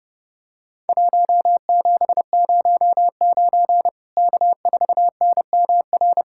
Morse code, heard as a keyed tone, about 30 wpm, 700 Hz. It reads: 1709 K4NMR
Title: Worked All Europe DX Contest - QTC Exchange - 30wpm